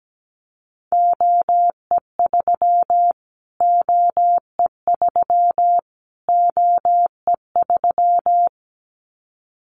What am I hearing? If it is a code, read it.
Morse code, heard as a keyed tone, about 17 wpm, 700 Hz. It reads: OE3 OE3 OE3
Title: Callsign Prefixes for Top 25 DXCC - Repeated 3x - 17wpm